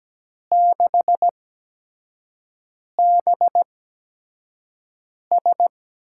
Morse code, heard as a keyed tone, about 17 wpm, spaced as for 5 wpm, 700 Hz. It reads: 6BS
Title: USA Call Signs - 4 Characters - 8x Character Spacing - 17wpm